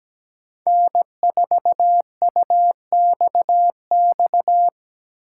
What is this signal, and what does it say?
Morse code, heard as a keyed tone, about 17 wpm, 700 Hz. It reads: N4UXX
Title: USA Call Signs - 3x Character Spacing - 17wpm